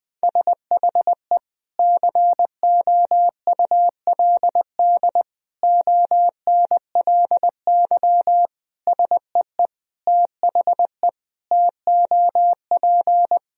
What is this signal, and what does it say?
Morse code, heard as a keyed tone, about 21 wpm, 700 Hz. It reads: SHE COULD ONLY SEE THE TOP
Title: Sentences from Top 1000 Words - Review Part 2 - 20wpm